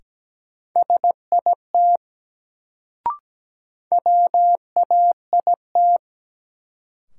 Morse code, heard as a keyed tone, about 17 wpm, 700 Hz. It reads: SIT WAIT
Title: Sentences from Top 500 Words - New Word Drill - 17wpm